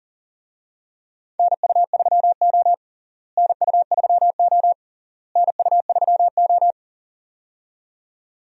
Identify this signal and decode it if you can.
Morse code, heard as a keyed tone, about 40 wpm, 700 Hz. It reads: NU3O NU3O NU3O
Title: USA Call Signs - 4 Characters - Repeated 3x - 40wpm